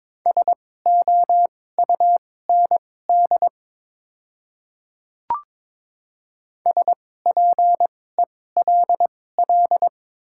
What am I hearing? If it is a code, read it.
Morse code, heard as a keyed tone, about 22 wpm, spaced as for 16 wpm, 700 Hz. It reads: SOUND SPELL
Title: 2x Character Spacing - Top 500 Words - 22wpm